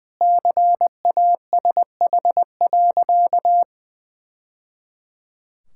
Morse code, heard as a keyed tone, about 20 wpm, 700 Hz. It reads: CASH.
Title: Sentences from Top 1000 Words - Review Part 5 - 20wpm